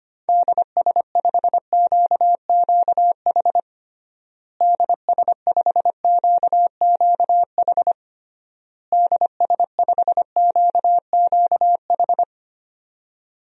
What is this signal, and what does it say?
Morse code, heard as a keyed tone, about 25 wpm, 700 Hz. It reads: DS5QQH DS5QQH DS5QQH
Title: DXCC-100 Call Signs - 6 Characters - Repeated 3x - 25wpm